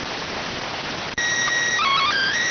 SSTV (slow-scan television), PD 290.